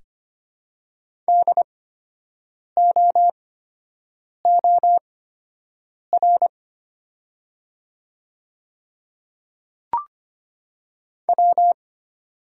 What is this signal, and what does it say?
Morse code, heard as a keyed tone, about 25 wpm, spaced as for 7 wpm, 700 Hz. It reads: DOOR W
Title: 8x Character Spacing - Top 300 Words - 25wpm